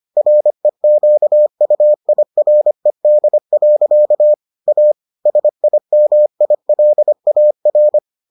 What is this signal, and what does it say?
Morse code, heard as a keyed tone, about 25 wpm, 600 Hz. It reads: REQUIRED. A SIMILAR